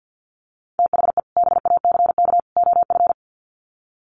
Morse code, heard as a keyed tone, about 48 wpm, 700 Hz. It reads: THE BACK OF
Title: Sets of 3 Words 50wpm